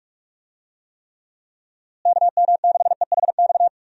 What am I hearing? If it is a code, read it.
Morse code, heard as a keyed tone, about 46 wpm, 700 Hz. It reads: KM6EHX
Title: USA Call Signs - 6 Characters - Repeated 2x - 45wpm